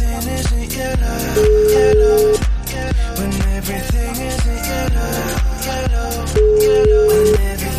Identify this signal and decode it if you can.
Ringback tone (Europe, 425 Hz), with noise.